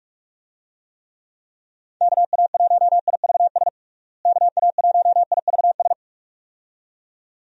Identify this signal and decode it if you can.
Morse code, heard as a keyed tone, about 41 wpm, 700 Hz. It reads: KA1IVS KA1IVS
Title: USA Call Signs - 6 Characters - Repeated 2x - 45wpm